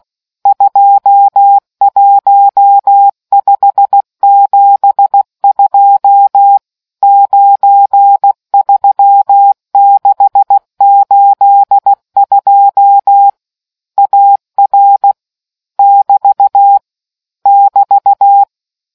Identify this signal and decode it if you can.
Morse code, heard as a keyed tone, about 16 wpm, 800 Hz. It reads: 21572 93682 AR <BT> <BT>